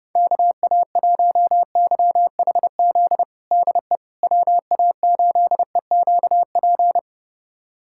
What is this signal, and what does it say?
Morse code, heard as a keyed tone, about 30 wpm, 700 Hz. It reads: KA1YHZ DE WA8EQP
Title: USA Callsign DE Callsign - 6 Characters - 30wpm